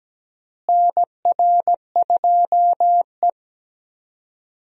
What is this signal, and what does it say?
Morse code, heard as a keyed tone, about 17 wpm, 700 Hz. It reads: NR2E
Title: USA Call Signs - 4 Characters - Repeated 3x - 17wpm